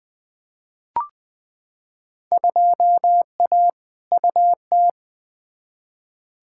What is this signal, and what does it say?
Morse code, heard as a keyed tone, about 20 wpm, 700 Hz. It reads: 2A UT
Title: Field Day Exchanges - 20wpm